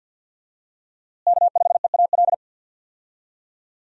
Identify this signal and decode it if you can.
Morse code, heard as a keyed tone, about 47 wpm, 700 Hz. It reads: K5EAL
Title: USA Call Signs - 5 Characters - Repeated 3x - 50wpm